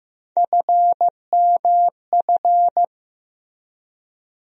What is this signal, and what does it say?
Morse code, heard as a keyed tone, about 15 wpm, 700 Hz. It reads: FMF
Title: USA Call Sign Postfixes - 3 Letters 15wpm